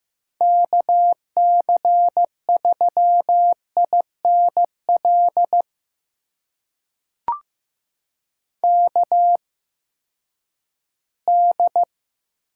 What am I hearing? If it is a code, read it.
Morse code, heard as a keyed tone, about 15 wpm, spaced as for 10 wpm, 700 Hz. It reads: KC3INL KD
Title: USA Call Signs - 6 Characters - 8x Character Spacing - 15wpm